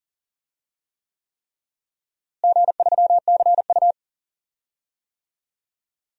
Morse code, heard as a keyed tone, about 40 wpm, 700 Hz. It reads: G3CU
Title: DXCC-100 Call Signs - 4 Characters - Repeated 3x - 40wpm